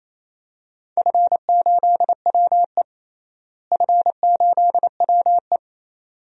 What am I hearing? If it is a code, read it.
Morse code, heard as a keyed tone, about 28 wpm, 700 Hz. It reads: F8WE F8WE
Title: DXCC-100 Call Signs - 4 Characters - Repeated 2x - 28wpm